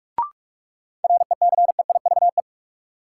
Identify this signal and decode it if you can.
Morse code, heard as a keyed tone, about 46 wpm, 700 Hz. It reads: RECEIVE
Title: Sentences from Top 1000 Words - New Word Drill - 45wpm